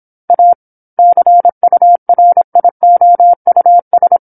Morse code, heard as a keyed tone, about 27 wpm, 700 Hz. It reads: A CURIOUS
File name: 08-Chapter-08 26 WPM